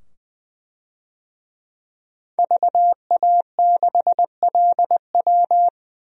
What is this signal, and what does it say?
Morse code, heard as a keyed tone, about 20 wpm, 700 Hz. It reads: VA6LW
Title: DXCC-100 Call Signs - 5 Characters - 20wpm